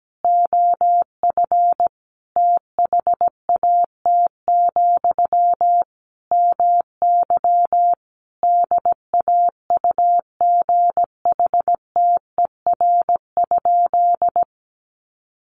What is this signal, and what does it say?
Morse code, heard as a keyed tone, about 17 wpm, 700 Hz. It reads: OF THAT, MY DAUGHTER?
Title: Sentences from Top 1500 Words 17wpm